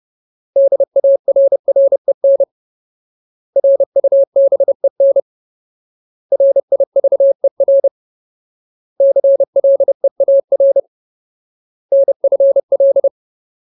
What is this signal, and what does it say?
Morse code, heard as a keyed tone, about 30 wpm, 550 Hz. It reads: DARREN RUBEN RIVER CLEAR NFL
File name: Ultimate Word List / 30WPM Ultimate